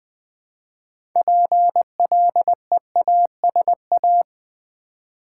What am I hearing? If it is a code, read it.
Morse code, heard as a keyed tone, about 20 wpm, 700 Hz. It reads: PLEASA
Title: Build A Word - 2 Letters At A Time - 8-Letter Words - 20wpm